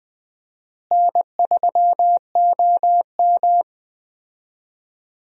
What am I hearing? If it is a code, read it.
Morse code, heard as a keyed tone, about 20 wpm, 700 Hz. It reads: N3OM
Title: USA Call Signs - 4 Characters - Repeated 3x - 20wpm